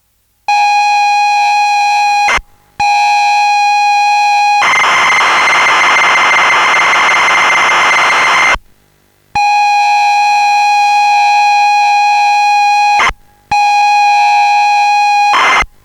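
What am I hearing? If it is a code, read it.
Computer data on cassette.
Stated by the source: ZX Spectrum game loading.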